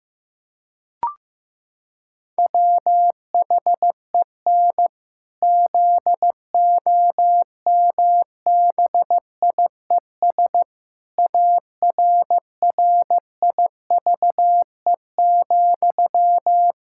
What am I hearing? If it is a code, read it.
Morse code, heard as a keyed tone, about 15 wpm, 700 Hz. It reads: WHEN ZOMBIES ARRIVE,
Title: Pangram Challenge v2 - 15wpm